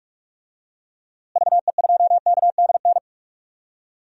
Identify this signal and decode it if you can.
Morse code, heard as a keyed tone, about 46 wpm, 700 Hz. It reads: VE2KDN
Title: DXCC-100 Call Signs - 6 Characters - Repeated 3x - 45wpm